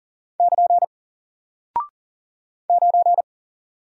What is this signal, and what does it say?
Morse code, heard as a keyed tone, about 40 wpm, 700 Hz. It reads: <KN> 9
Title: Single Character (Letter:Number:Punctuation:Prosign) Alt 40wpm